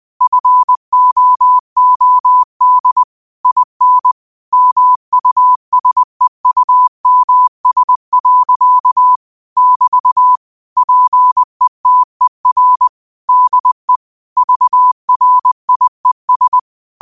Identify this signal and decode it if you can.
Morse code, heard as a keyed tone, about 20 wpm, 1000 Hz. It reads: FOOD IN MUSEUMS. <BT> PETER DE VRIES
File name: qotdfiles / 20wpm